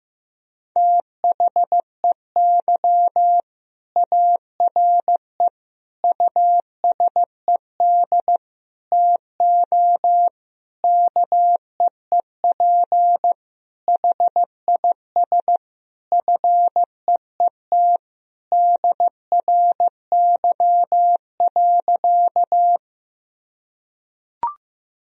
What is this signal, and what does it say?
Morse code, heard as a keyed tone, about 15 wpm, 700 Hz. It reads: THEY ARE USED TO KEEP HIS FEET DRY.
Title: Sentences from Top 1000 Words - Review Part 3 - 15wpm